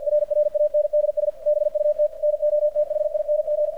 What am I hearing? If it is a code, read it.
Morse code, heard as a keyed tone, about 53 wpm, 600 Hz, with radio noise. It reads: RRANDS BUT NOTHING
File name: archived / CRSnet